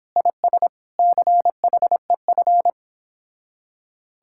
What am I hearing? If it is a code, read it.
Morse code, heard as a keyed tone, about 29 wpm, 700 Hz. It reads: IS CHEF